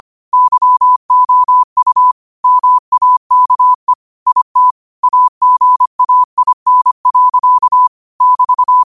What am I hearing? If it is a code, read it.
Morse code, heard as a keyed tone, about 25 wpm, 1000 Hz. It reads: YOU MAKE IT AGAIN. <BT>
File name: qotdfiles / 25wpm